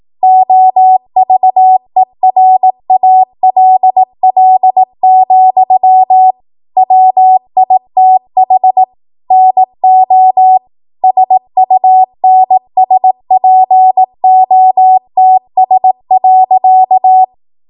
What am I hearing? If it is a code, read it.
Morse code, heard as a keyed tone, about 18 wpm, 750 Hz. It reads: OVERALL, WITH NO SUNSPOTS.
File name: Archive / 18 WPM / 18WPM